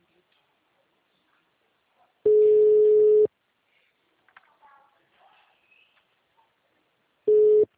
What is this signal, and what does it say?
Ringback tone (Europe, 425 Hz).
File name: out- -2019- - - .651